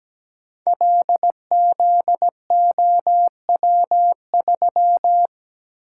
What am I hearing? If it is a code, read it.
Morse code, heard as a keyed tone, about 17 wpm, 700 Hz. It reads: LZOW3
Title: Code Groups - 5 Letter:Numbers - 17wpm